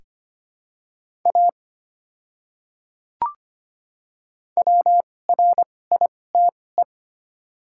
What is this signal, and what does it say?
Morse code, heard as a keyed tone, about 25 wpm, spaced as for 18 wpm, 690 Hz. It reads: A WRITE
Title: 2x Character Spacing - Top 300 Words - 25wpm